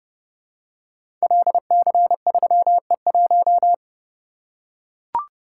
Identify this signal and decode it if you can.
Morse code, heard as a keyed tone, about 29 wpm, 700 Hz. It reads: LC3E1
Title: Code Groups - 5 Letter:Numbers - 30wpm